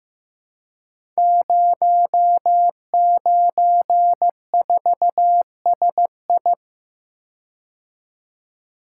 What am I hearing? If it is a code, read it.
Morse code, heard as a keyed tone, about 15 wpm, 700 Hz. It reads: 094SI
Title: Code Groups - 5 Letter:Numbers - 15wpm